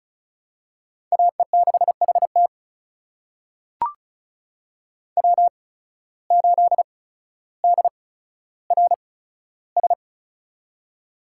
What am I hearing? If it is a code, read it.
Morse code, heard as a keyed tone, about 36 wpm, spaced as for 15 wpm, 700 Hz. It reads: AE6HT W8DRS
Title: USA Call Signs - 5 Characters - 8x Character Spacing - 35wpm